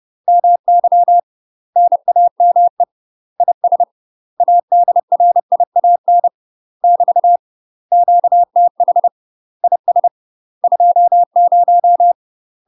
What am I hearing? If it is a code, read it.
Morse code, heard as a keyed tone, about 30 wpm, 700 Hz. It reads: MY NAME IS ADRIAN <BT> QTH IS 20
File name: UK MASTER QSOs TRAINING / 30WPM / UK MASTER QSOs 0044